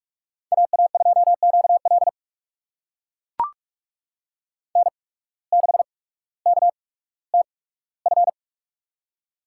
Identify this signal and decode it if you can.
Morse code, heard as a keyed tone, about 43 wpm, spaced as for 19 wpm, 700 Hz. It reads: AA2QL N6KTF
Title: USA Call Signs - 5 Characters - 8x Character Spacing - 45wpm